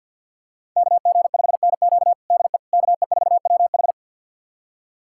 Morse code, heard as a keyed tone, about 51 wpm, 700 Hz. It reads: KG5NQ DE KE4WH
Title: USA Callsign DE Callsign 50wpm